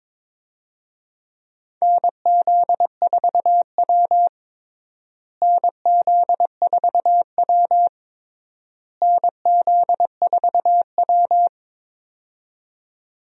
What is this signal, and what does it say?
Morse code, heard as a keyed tone, about 21 wpm, 700 Hz. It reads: NZ4W NZ4W NZ4W
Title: USA Call Signs - 4 Characters - Repeated 3x - 22wpm